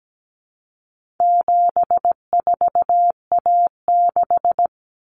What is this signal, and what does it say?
Morse code, heard as a keyed tone, about 17 wpm, 700 Hz. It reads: 74A6
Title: Common Cut Numbers - 4-Digit 17wpm